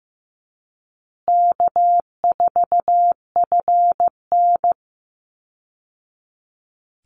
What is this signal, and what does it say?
Morse code, heard as a keyed tone, about 15 wpm, 700 Hz. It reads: K4FN
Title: Easy Call Signs Repeated 3x - 15wpm